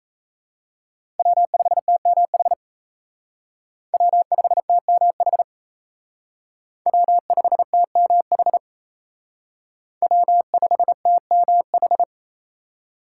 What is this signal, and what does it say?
Morse code, heard as a keyed tone, about 33 wpm, 700 Hz. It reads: W5TMH W5TMH W5TMH W5TMH
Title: Speed Racing - USA Call Signs 28wpm